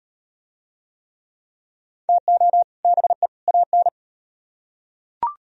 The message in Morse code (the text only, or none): TO BE AN